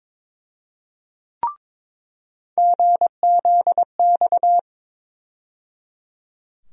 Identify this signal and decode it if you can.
Morse code, heard as a keyed tone, about 22 wpm, 700 Hz. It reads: GZX
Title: USA Call Sign Postfixes - 3 Letters 22wpm